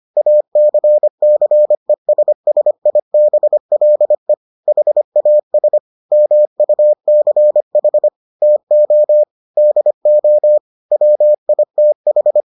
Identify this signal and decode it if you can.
Morse code, heard as a keyed tone, about 25 wpm, 600 Hz. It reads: ACCESSIBLE HAS MUCH TO DO WITH